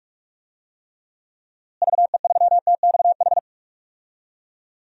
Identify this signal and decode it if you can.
Morse code, heard as a keyed tone, about 47 wpm, 700 Hz. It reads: VE3TXH
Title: DXCC-100 Call Signs - 6 Characters - Repeated 3x - 45wpm